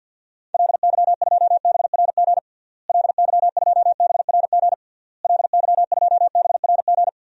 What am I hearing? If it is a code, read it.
Morse code, heard as a keyed tone, about 53 wpm, 700 Hz. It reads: LY2BRG LY2BRG LY2BRG
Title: DXCC-100 Call Signs - 6 Characters - Repeated 3x - 50wpm